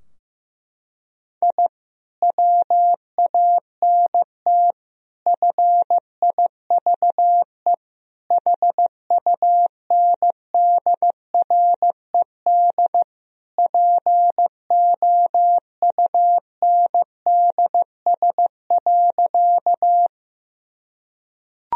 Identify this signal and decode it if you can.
Morse code, heard as a keyed tone, about 15 wpm, 700 Hz. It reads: I WANT FIVE HUNDRED POUNDS.
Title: Sentences from Top 1000 Words - Review Part 2 - 15wpm